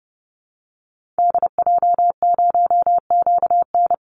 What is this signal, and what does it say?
Morse code, heard as a keyed tone, about 29 wpm, 700 Hz. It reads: DJ0QN
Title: Call Signs Repeated 2x - 30wpm